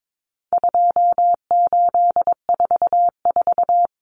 Morse code, heard as a keyed tone, about 22 wpm, 700 Hz. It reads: 2844
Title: Numbers - 4-Digit 22wpm